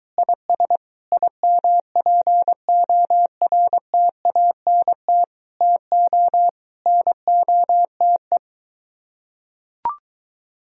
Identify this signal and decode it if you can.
Morse code, heard as a keyed tone, about 22 wpm, 700 Hz. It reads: IS IMPORTANT TO NOTE